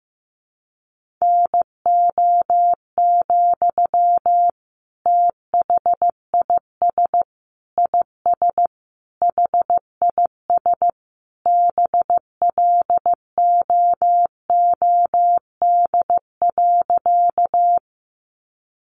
Morse code, heard as a keyed tone, about 15 wpm, 700 Hz. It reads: NO, THIS IS HIS BLOOD.
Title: Sentences from Top 1100 Words 15wpm